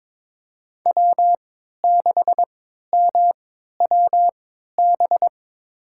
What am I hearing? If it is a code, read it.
Morse code, heard as a keyed tone, about 22 wpm, spaced as for 12 wpm, 700 Hz. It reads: W6MWB
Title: USA Call Signs - 5 Characters - 3x Character Spacing - 22wpm